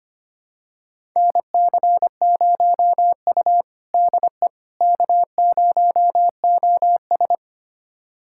Morse code, heard as a keyed tone, about 25 wpm, 700 Hz. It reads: NC0U DE K0OS
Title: USA Callsign DE Callsign - 4 Characters - 25wpm